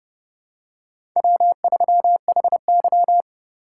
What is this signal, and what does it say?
Morse code, heard as a keyed tone, about 30 wpm, 700 Hz. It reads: W3HY
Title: USA Call Signs - 2x Character Spacing - 30wpm